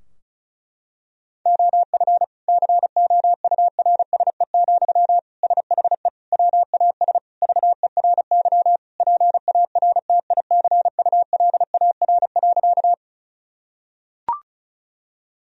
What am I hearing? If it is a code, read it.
Morse code, heard as a keyed tone, about 36 wpm, 700 Hz. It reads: OF COURSE, SHE WAS VERY PARTICULAR.
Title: Sentences from Top 1000 Words - Review Part 4 - 35wpm